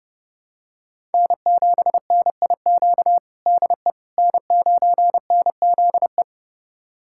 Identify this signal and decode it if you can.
Morse code, heard as a keyed tone, about 32 wpm, 700 Hz. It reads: N7NIQ DE N9NZE
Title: USA Callsign DE Callsign - 5 Characters - 30wpm